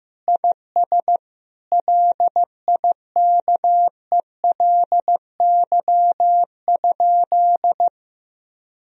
Morse code, heard as a keyed tone, about 15 wpm, 700 Hz. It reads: IS LIKELY?
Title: Sentences from Top 1000 Words - Review Part 4 - 15wpm